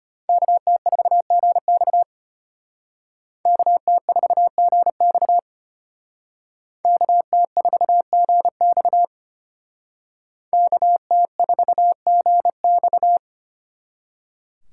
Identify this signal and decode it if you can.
Morse code, heard as a keyed tone, about 34 wpm, 700 Hz. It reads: KT4GX KT4GX KT4GX KT4GX